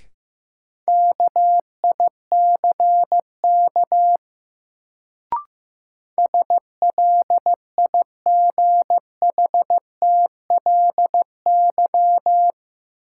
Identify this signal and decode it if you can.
Morse code, heard as a keyed tone, about 15 wpm, 700 Hz. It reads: KICK SLIGHTLY
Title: Sentences from Top 1300 Words - New Word Drill - 15wpm